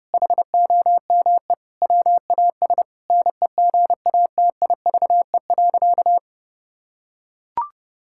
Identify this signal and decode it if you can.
Morse code, heard as a keyed tone, about 30 wpm, 700 Hz. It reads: HOME WAS NEGATIVE.
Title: Sentences from Top 1000 Words - Review Part 6 - 30wpm